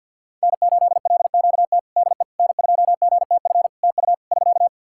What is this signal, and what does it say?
Morse code, heard as a keyed tone, about 51 wpm, 700 Hz. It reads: N8LQT DE N2GTF TU <SK>
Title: K1USN SST QSO Practice 50wpm